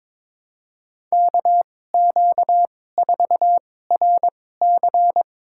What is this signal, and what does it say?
Morse code, heard as a keyed tone, about 22 wpm, spaced as for 16 wpm, 700 Hz. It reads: KQ4RC
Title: USA Call Signs - 5 Characters - 2x Character Spacing - 22wpm